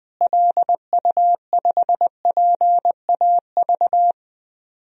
Morse code, heard as a keyed tone, about 20 wpm, 700 Hz. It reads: LU5PAV
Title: DXCC-100 Call Signs - 6 Characters - Repeated 2x - 20wpm